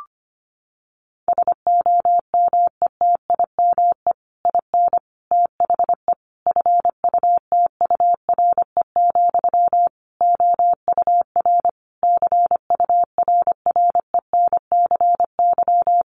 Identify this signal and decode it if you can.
Morse code, heard as a keyed tone, about 25 wpm, 700 Hz. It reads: SOMETIME IN THE FUTURE, OUR CURRENCY
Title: Sentences from Top 1500 Words 25wpm